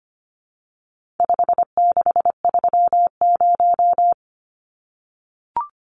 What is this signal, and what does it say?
Morse code, heard as a keyed tone, about 25 wpm, 700 Hz. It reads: 5630
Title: Numbers - 4-Digit 25wpm